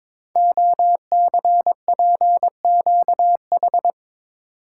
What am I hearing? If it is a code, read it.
Morse code, heard as a keyed tone, about 22 wpm, 700 Hz. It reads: OCPQH
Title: Code Groups - 5 Letter:Numbers - 22wpm